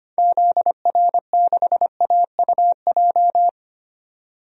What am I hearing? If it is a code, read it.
Morse code, heard as a keyed tone, about 25 wpm, 700 Hz. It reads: ZR6AUJ
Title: DXCC-100 Call Signs - 6 Characters - 25wpm